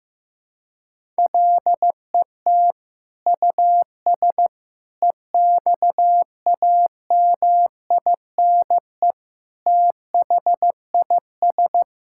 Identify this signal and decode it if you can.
Morse code, heard as a keyed tone, about 15 wpm, 700 Hz. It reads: LET US EXAMINE THIS
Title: Sentences from Top 1600 Words 15wpm